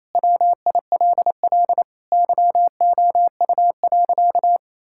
Morse code, heard as a keyed tone, about 28 wpm, 700 Hz. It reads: WILL YOU.
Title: Sentences from Top 1000 Words - Review Part 6 - 28wpm